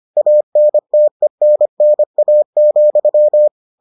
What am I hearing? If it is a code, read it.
Morse code, heard as a keyed tone, about 25 wpm, 600 Hz. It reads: ANTENNA,